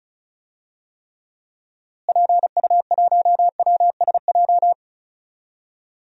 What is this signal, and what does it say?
Morse code, heard as a keyed tone, about 34 wpm, 700 Hz. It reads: PU1WSJ
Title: DXCC-100 Call Signs - 6 Characters - Repeated 3x - 35wpm